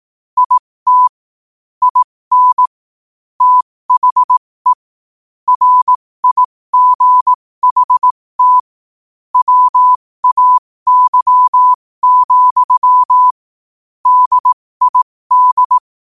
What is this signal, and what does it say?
Morse code, heard as a keyed tone, about 18 wpm, spaced as for 15 wpm, 1000 Hz. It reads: IT IN THE RIGHT WAY, DID
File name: qotdfiles / 15wpm